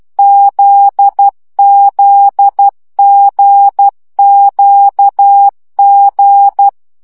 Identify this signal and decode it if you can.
Morse code, heard as a keyed tone, about 12 wpm, 800 Hz. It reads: ZZGQG
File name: CW gyakorlas g z q 10 baud